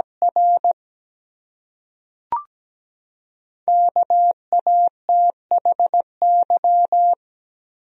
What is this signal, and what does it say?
Morse code, heard as a keyed tone, about 17 wpm, 700 Hz. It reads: R KATHY
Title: USA Common Names - Max 5 Letters - 17wpm